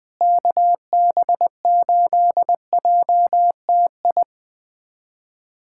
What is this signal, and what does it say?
Morse code, heard as a keyed tone, about 20 wpm, 700 Hz. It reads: KB8JTI
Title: USA Callsign DE Callsign 20wpm